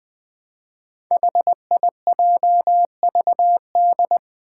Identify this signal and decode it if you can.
Morse code, heard as a keyed tone, about 20 wpm, 700 Hz. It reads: HIJVD
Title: Five Challenging Letter-Number Combinations 20wpm